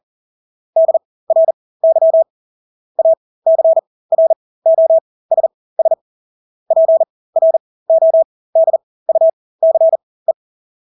Morse code, heard as a keyed tone, about 40 wpm, spaced as for 20 wpm, 650 Hz. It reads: DRY ACROSS PRODUCE
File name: Engelske ord 40-20 wpm